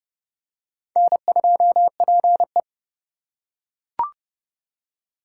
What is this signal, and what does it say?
Morse code, heard as a keyed tone, about 29 wpm, 700 Hz. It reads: N2PE